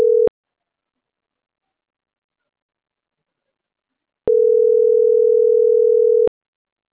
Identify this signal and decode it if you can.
Ringback tone (North America), 440 + 480 Hz.